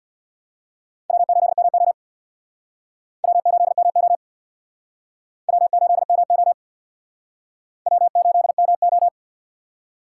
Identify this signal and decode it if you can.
Morse code, heard as a keyed tone, about 62 wpm, 700 Hz. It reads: W8MO W8MO W8MO W8MO
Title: Speed Racing - USA 1x2 Call Signs 50wpm